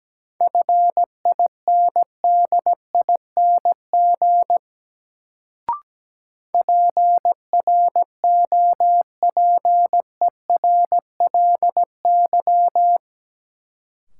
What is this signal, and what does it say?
Morse code, heard as a keyed tone, about 17 wpm, 700 Hz. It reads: FINDING PROPERLY
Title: Sentences from Top 1100 Words - New Word Drill Alt - 17wpm